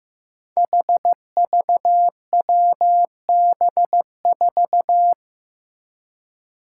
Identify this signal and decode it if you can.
Morse code, heard as a keyed tone, about 16 wpm, 700 Hz. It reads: HVWB4
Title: Five Challenging Letter-Number Combinations 15wpm